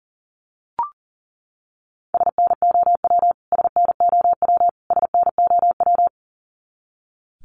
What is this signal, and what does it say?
Morse code, heard as a keyed tone, about 40 wpm, 700 Hz. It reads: SNOW SNOW SNOW
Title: Top 500 Words Repeated 40wpm